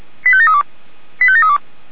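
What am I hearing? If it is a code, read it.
Five-tone selective call (ZVEI), heard as quick runs of five tones, tones of about 70 ms. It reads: 85732 85732
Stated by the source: ZVEI five-tone selective call received by SDR.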